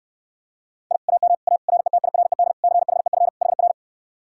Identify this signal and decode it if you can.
Morse code, heard as a keyed tone, about 69 wpm, 700 Hz. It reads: I AM A RETIRED CHEF HR